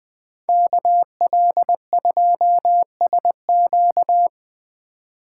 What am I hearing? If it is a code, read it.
Morse code, heard as a keyed tone, about 20 wpm, 700 Hz. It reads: KL2SQ
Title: Speed Racing - USA 2x2 Call Signs 17wpm